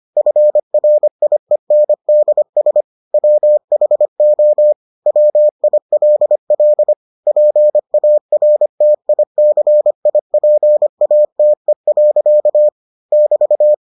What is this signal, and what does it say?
Morse code, heard as a keyed tone, about 25 wpm, 600 Hz. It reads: FRIENDS WHO WILL PARTICIPATE. <BT>